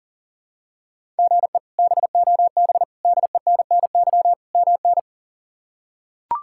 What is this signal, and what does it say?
Morse code, heard as a keyed tone, about 40 wpm, 700 Hz. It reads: GE BOB DENNY MN
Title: K1USN SST QSO Practice - 40wpm